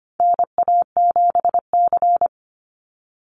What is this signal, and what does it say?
Morse code, heard as a keyed tone, about 25 wpm, 700 Hz. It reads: NA7C
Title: Call Signs - 25wpm